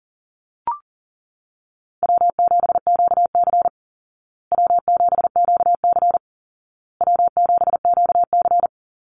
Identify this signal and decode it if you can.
Morse code, heard as a keyed tone, about 40 wpm, 700 Hz. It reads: W7QC W7QC W7QC
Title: Call Signs Repeated 2x - 40wpm